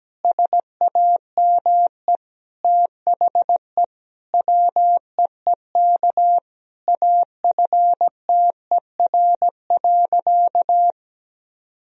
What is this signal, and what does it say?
Morse code, heard as a keyed tone, about 17 wpm, 700 Hz. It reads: SAME THE WEEK AFTER.
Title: Sentences from Top 500 Words and Abbreviations 17wpm